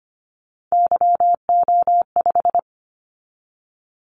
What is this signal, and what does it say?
Morse code, heard as a keyed tone, about 25 wpm, 700 Hz. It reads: YO5
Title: Callsign Prefixes for Top 25 DXCC - 25wpm